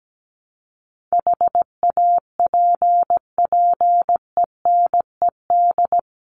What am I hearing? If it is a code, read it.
Morse code, heard as a keyed tone, about 17 wpm, 700 Hz. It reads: HAPPENED
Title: ED Suffix - Top 100 Words with Base Words 17wpm